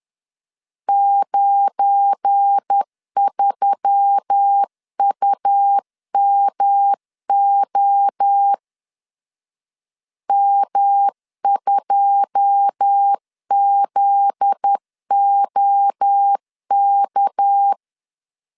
Morse code, heard as a keyed tone, about 10 wpm, 800 Hz. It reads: 93UMO M2ZOK
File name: LEZIONI / 9 2 10wpm 2736